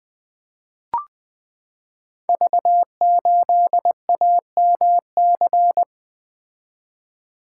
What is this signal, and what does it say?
Morse code, heard as a keyed tone, about 20 wpm, 700 Hz. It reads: V8AMC
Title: Code Groups - 5 Letter:Numbers - 20wpm